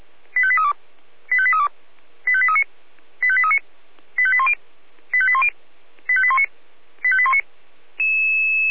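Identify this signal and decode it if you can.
Five-tone selective call (ZVEI), heard as quick runs of five tones, tones of about 70 ms. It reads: 85832 85832 85839 85839 85810 85810 85819 85819
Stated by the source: ZVEI five-tone selective call received by SDR.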